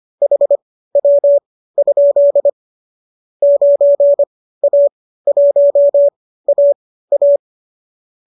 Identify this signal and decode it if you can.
Morse code, heard as a keyed tone, about 25 wpm, spaced as for 15 wpm, 570 Hz. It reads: HW? 9A1AA
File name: qso219 15